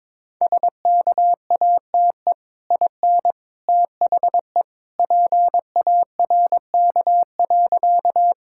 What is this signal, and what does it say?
Morse code, heard as a keyed tone, about 22 wpm, 700 Hz. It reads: SKATE IN THE PARK.
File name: practice / 22wpm / Easy Sentences with Hints - 22wpm